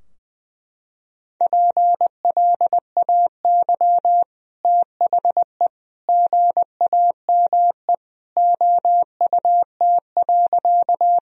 Morse code, heard as a keyed tone, about 21 wpm, 700 Hz. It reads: PLAY THE GAME OUT.
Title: Sentences from Top 1000 Words - Review Part 3 - 20wpm